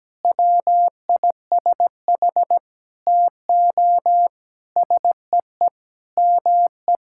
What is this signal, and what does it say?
Morse code, heard as a keyed tone, about 17 wpm, 700 Hz. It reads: WISH TO SEE ME
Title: Sentences from Top 1600 Words 17wpm